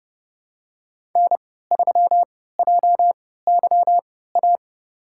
Morse code, heard as a keyed tone, about 30 wpm, spaced as for 17 wpm, 700 Hz. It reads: N3JYA
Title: USA Call Signs - 5 Characters - 3x Character Spacing - 30wpm